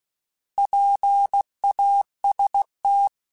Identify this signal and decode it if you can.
Morse code, heard as a keyed tone, about 16 wpm, 800 Hz. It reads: PAST
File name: Alphabetical / Alphabetical mp3s / Ps @15